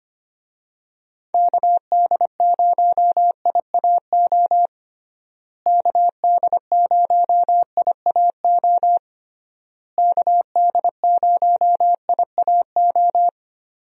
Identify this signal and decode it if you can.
Morse code, heard as a keyed tone, about 25 wpm, 700 Hz. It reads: KD0IAO KD0IAO KD0IAO
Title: USA Call Signs - 6 Characters - Repeated 3x - 25wpm